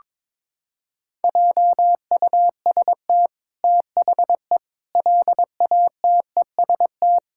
Morse code, heard as a keyed tone, about 22 wpm, 700 Hz. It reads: JUST THE LATEST
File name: practice / 22wpm / Sentences with 8-Word Phrases - 22wpm